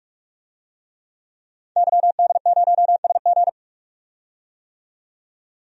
Morse code, heard as a keyed tone, about 43 wpm, 700 Hz. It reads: YD0SG
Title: DXCC-100 Call Signs - 5 Characters - 45wpm